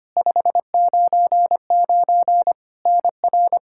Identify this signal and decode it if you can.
Morse code, heard as a keyed tone, about 25 wpm, 700 Hz. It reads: 599 NR
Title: Practice QSO - Exchange 1 - Beginning - 25wpm